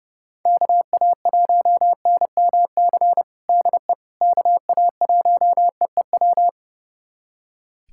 Morse code, heard as a keyed tone, about 30 wpm, 700 Hz. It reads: KA1NMC DE KA1EEW
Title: USA Callsign DE Callsign 30wpm